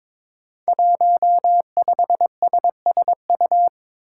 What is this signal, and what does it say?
Morse code, heard as a keyed tone, about 22 wpm, 700 Hz. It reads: 15SSU
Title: Five Challenging Letter-Number Combinations 22wpm